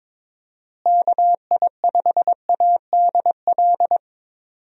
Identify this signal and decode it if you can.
Morse code, heard as a keyed tone, about 21 wpm, 700 Hz. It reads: KI5ADL
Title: USA Call Signs - 6 Characters - Repeated 3x - 22wpm